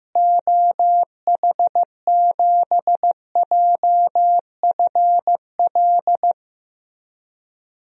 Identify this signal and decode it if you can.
Morse code, heard as a keyed tone, about 15 wpm, 700 Hz. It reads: OH7JFL
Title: DXCC-100 Call Signs - 6 Characters - Repeated 3x - 15wpm